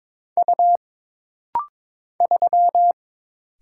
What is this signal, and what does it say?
Morse code, heard as a keyed tone, about 22 wpm, 700 Hz. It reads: U 3
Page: This rapid-fire timing is geared to push your ICR (Instant Character Recognition) skills.